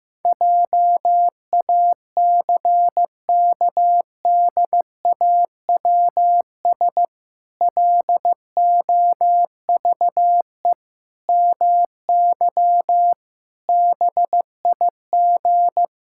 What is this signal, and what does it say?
Morse code, heard as a keyed tone, about 14 wpm, 700 Hz. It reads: JACKDAWS LOVE MY BIG
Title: Pangram Challenge v2 - 15wpm